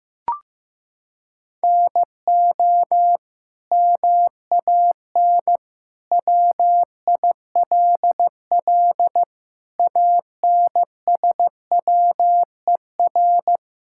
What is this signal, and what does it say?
Morse code, heard as a keyed tone, about 15 wpm, 700 Hz. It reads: NO MAN WILL ANSWER
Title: Sentences from Top 1000 Words - Review Part 6 - 15wpm